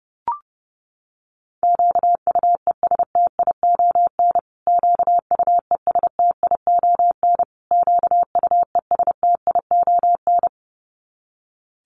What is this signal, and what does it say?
Morse code, heard as a keyed tone, about 28 wpm, 700 Hz. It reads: QUESTION QUESTION QUESTION
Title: Top 500 Words Repeated 30wpm